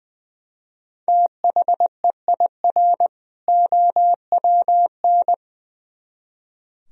Morse code, heard as a keyed tone, about 22 wpm, 700 Hz. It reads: THEIR OWN
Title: Speed-Racing - Sets of 2 Words 20wpm